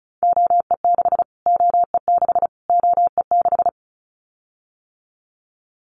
Morse code, heard as a keyed tone, about 34 wpm, 700 Hz. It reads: OE6 OE6 OE6
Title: Callsign Prefixes for Top 25 DXCC - Repeated 3x - 35wpm